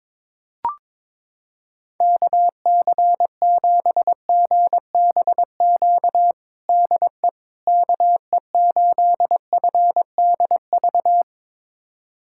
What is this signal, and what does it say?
Morse code, heard as a keyed tone, about 22 wpm, 700 Hz. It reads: KC7GBQ DE KE8FDV
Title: USA Callsign DE Callsign - 6 Characters - 22wpm